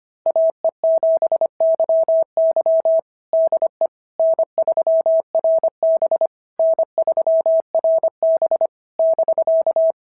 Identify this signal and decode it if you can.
Morse code, heard as a keyed tone, about 25 wpm, 650 Hz. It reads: AE7YY DE N3RB N3RB <BK>
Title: SOTA QSO Practice 25wpm